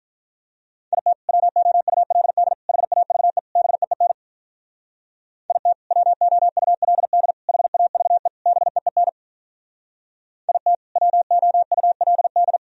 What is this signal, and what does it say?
Morse code, heard as a keyed tone, about 44 wpm, 700 Hz. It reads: IT WOULD HAVE BEEN IT WOULD HAVE BEEN IT WOULD